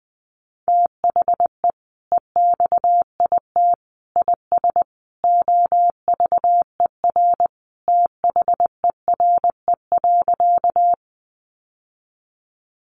Morse code, heard as a keyed tone, about 20 wpm, 700 Hz. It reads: THE EXIT IS OVER THERE.
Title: Sentences from Top 1100 Words 20wpm